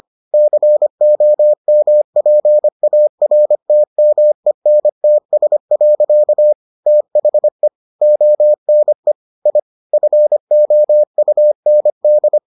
Morse code, heard as a keyed tone, about 25 wpm, 600 Hz. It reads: COMPARTMENTS. THE ONE I FOUND